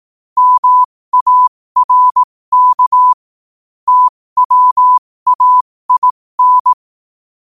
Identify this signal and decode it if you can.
Morse code, heard as a keyed tone, about 18 wpm, spaced as for 15 wpm, 1000 Hz. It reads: MARK TWAIN
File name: qotdfiles / 15wpm